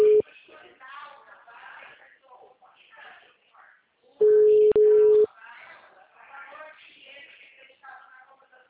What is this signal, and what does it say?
Ringback tone (Europe, 425 Hz), with speech.